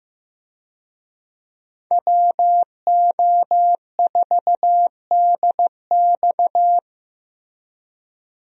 Morse code, heard as a keyed tone, about 15 wpm, 700 Hz. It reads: WO4DX
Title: USA Call Signs - 5 Characters - Repeated 2x - 15wpm